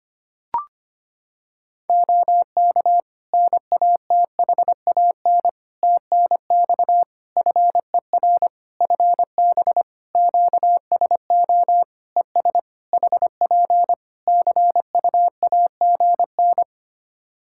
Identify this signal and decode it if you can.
Morse code, heard as a keyed tone, about 25 wpm, 700 Hz. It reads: OK NATHAN TNX FER FB QSO ES HP CUAGN
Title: Practice QSO - Exchange 4 - Ending - 25wpm